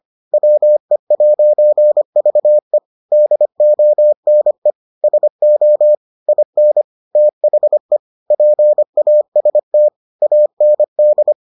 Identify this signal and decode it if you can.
Morse code, heard as a keyed tone, about 25 wpm, 600 Hz. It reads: WE'VE DONE SO IN THE PAST AND